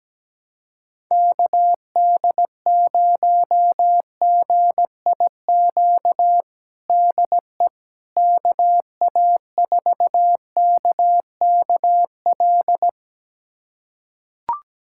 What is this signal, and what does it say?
Morse code, heard as a keyed tone, about 17 wpm, 700 Hz. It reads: KD0GIQ DE KA4KKL
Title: USA Callsign DE Callsign - 6 Characters - 17wpm